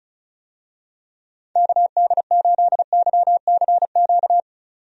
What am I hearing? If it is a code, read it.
Morse code, heard as a keyed tone, about 36 wpm, 700 Hz. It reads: KD8YCQ
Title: USA Call Signs - 6 Characters - 35wpm